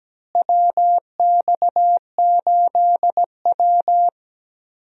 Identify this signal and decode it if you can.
Morse code, heard as a keyed tone, about 17 wpm, 700 Hz. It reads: WX8W
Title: USA Call Signs - 4 Characters - Repeated 3x - 17wpm